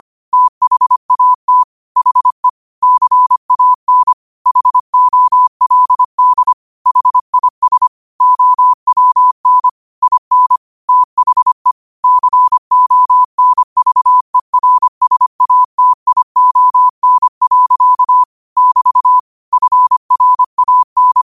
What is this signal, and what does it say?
Morse code, heard as a keyed tone, about 25 wpm, 1000 Hz. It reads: THAT HE CAN HOLD HIS OWN IN THE CONVERSATION. <BT> FRAN
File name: qotdfiles / 25wpm